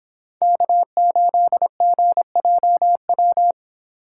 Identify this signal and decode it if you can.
Morse code, heard as a keyed tone, about 24 wpm, 700 Hz. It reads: K8GJW